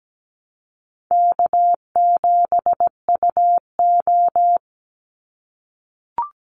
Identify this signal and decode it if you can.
Morse code, heard as a keyed tone, about 17 wpm, 700 Hz. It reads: K7UO
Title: USA Call Signs 17wpm